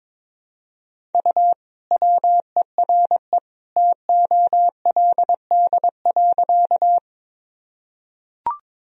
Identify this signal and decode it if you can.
Morse code, heard as a keyed tone, about 22 wpm, 700 Hz. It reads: U WERE TOLD.
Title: Sentences from Top 400 Words and Abbreviations 22wpm